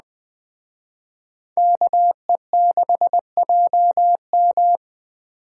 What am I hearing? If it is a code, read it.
Morse code, heard as a keyed tone, about 23 wpm, 700 Hz. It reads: KE6JM